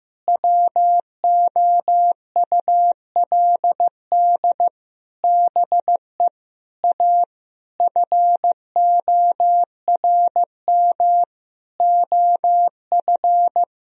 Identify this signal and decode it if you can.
Morse code, heard as a keyed tone, about 15 wpm, 700 Hz. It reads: WOULD BE A FORM OF
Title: Sentences from Top 1000 Words - Review Part 1 - 15wpm